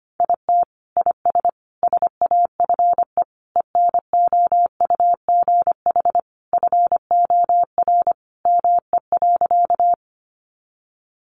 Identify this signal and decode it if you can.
Morse code, heard as a keyed tone, about 25 wpm, 700 Hz. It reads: IT IS SAFE ENOUGH FOR ME.
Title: Sentences from Top 900 Words 25wpm